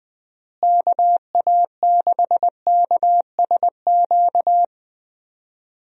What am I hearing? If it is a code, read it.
Morse code, heard as a keyed tone, about 20 wpm, 700 Hz. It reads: KA6KSQ
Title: USA Call Signs - 6 Characters - Repeated 3x - 20wpm